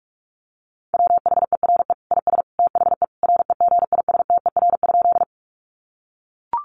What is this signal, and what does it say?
Morse code, heard as a keyed tone, about 44 wpm, 700 Hz. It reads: WHERE IS THE REGISTER?
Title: Sentences from Top 1200 Words 45wpm